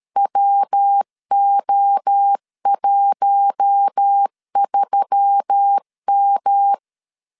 Morse code, heard as a keyed tone, about 13 wpm, 800 Hz. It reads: WO13M
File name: LEZIONI / 8 YJ45 / 8 YJ4 12wpm 2151